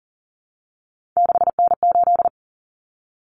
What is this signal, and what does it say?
Morse code, heard as a keyed tone, about 39 wpm, 700 Hz. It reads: 6N8
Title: Common Cut Numbers - 3-Digit 40wpm